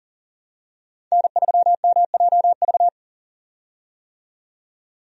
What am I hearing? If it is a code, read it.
Morse code, heard as a keyed tone, about 40 wpm, 700 Hz. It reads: N3MJV
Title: USA Call Signs - 5 Characters - 40wpm